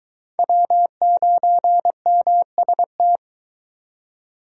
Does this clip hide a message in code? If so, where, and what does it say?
Yes: Morse code from 0.4 to 3.2 s, heard as a keyed tone, about 23 wpm, 700 Hz. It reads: W9MST